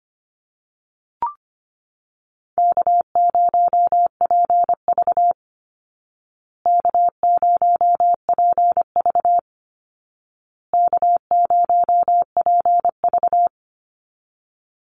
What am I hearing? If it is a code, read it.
Morse code, heard as a keyed tone, about 25 wpm, 700 Hz. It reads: K0PV K0PV K0PV
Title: Easy Call Signs Repeated 3x - 25wpm